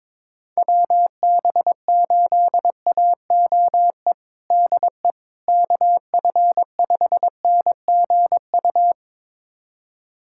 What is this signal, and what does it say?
Morse code, heard as a keyed tone, about 22 wpm, 700 Hz. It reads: WB8AOE DE KF5NGU
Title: USA Callsign DE Callsign - 6 Characters - 22wpm